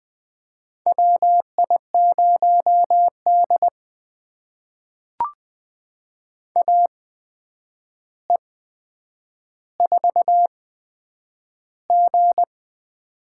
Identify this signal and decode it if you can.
Morse code, heard as a keyed tone, about 20 wpm, spaced as for 9 wpm, 700 Hz. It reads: WI0D AE4G
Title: USA Call Signs - 4 Characters - 8x Character Spacing - 20wpm